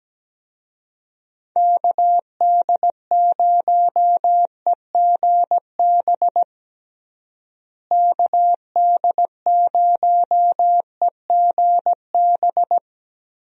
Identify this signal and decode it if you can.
Morse code, heard as a keyed tone, about 17 wpm, 700 Hz. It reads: KD0EGB KD0EGB
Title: USA Call Signs - 6 Characters - Repeated 2x - 17wpm